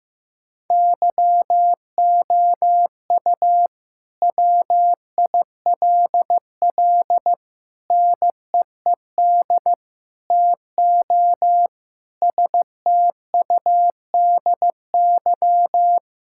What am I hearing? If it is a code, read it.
Morse code, heard as a keyed tone, about 16 wpm, 700 Hz. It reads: YOU WILL NEED TO STUDY
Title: Sentences from Top 1000 Words - Review Part 5 - 15wpm